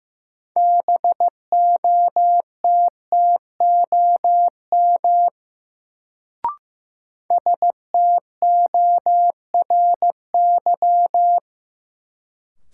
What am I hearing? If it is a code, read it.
Morse code, heard as a keyed tone, about 15 wpm, 700 Hz. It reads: BOTTOM STORY
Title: Sets of 3 Words - New Word Drill - 15wpm